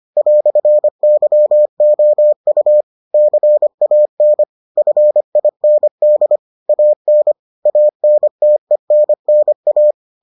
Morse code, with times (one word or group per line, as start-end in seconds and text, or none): 0.2-2.8 "YOU
3.1-4.4 CAN
4.8-6.4 FIND
6.7-7.3 AN
7.7-9.9 ANTENNA